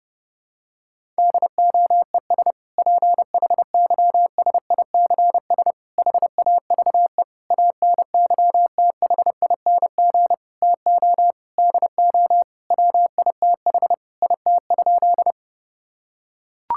Morse code, heard as a keyed tone, about 30 wpm, 700 Hz. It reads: DOES PHYSICS HAVE ANYTHING TO DO WITH IT?
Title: Sentences from Top 1000 Words - Review Part 5 - 30wpm